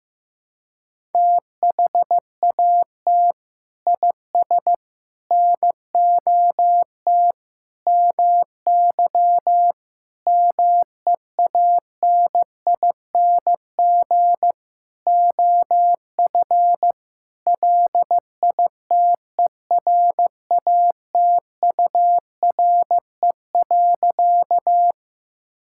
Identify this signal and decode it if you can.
Morse code, heard as a keyed tone, about 15 wpm, 700 Hz. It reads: THAT IS NOT MY MEANING OF LITERATURE.
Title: Sentences from Top 1000 Words - Review Part 6 - 15wpm